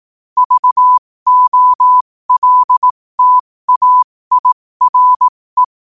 Morse code, heard as a keyed tone, about 18 wpm, spaced as for 15 wpm, 1000 Hz. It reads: VOLTAIRE
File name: qotdfiles / 15wpm